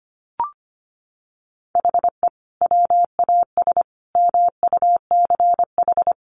Morse code, heard as a keyed tone, about 25 wpm, 700 Hz. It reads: HE WAS MUCH
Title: Sentences from Top 1200 Words 25wpm